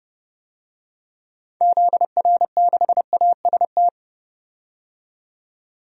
Morse code, heard as a keyed tone, about 30 wpm, 700 Hz. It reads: ZR6AST
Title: DXCC-100 Call Signs - 6 Characters - 30wpm